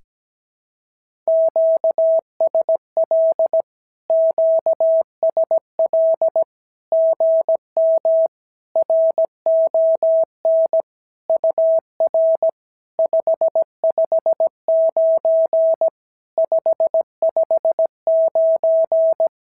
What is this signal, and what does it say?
Morse code, heard as a keyed tone, about 17 wpm, 650 Hz. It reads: QSL QSL GM RON UR 559 559
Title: SOTA QSO Practice 17wpm